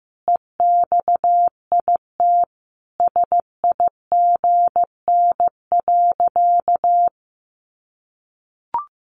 Morse code, heard as a keyed tone, about 15 wpm, 700 Hz. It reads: EXIT SIGN.
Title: Sentences from Top 1100 Words 15wpm